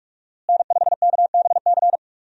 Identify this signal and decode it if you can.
Morse code, heard as a keyed tone, about 46 wpm, 700 Hz. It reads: N5KBC
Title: USA Call Signs - 5 Characters - Repeated 3x - 45wpm